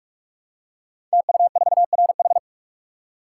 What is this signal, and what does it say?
Morse code, heard as a keyed tone, about 44 wpm, 700 Hz. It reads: TU4RH
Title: Code Groups - 5 Letter:Numbers - 45wpm